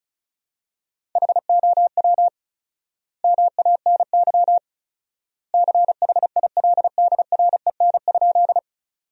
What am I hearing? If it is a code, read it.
Morse code, heard as a keyed tone, about 34 wpm, 700 Hz. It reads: HOW MANY CHILDREN?
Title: 4x Word-Spacing - Sentences from Top 300 Words - 35wpm